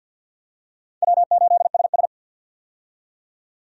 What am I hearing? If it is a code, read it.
Morse code, heard as a keyed tone, about 46 wpm, 700 Hz. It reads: W8SS
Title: USA Call Signs - 4 Characters - 50wpm